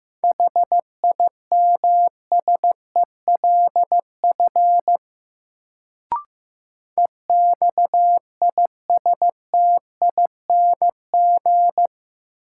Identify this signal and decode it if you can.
Morse code, heard as a keyed tone, about 15 wpm, 700 Hz. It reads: HIMSELF EXISTING
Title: Sentences from Top 1500 Words - New Word Drill Alt - 15wpm